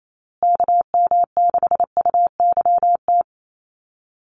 Morse code, heard as a keyed tone, about 28 wpm, 700 Hz. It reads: KM6UYT
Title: USA Call Signs Repeated 2x 28wpm